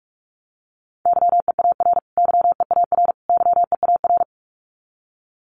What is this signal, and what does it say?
Morse code, heard as a keyed tone, about 44 wpm, 700 Hz. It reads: YEAR YEAR YEAR
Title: Top 200 Words Repeated 45wpm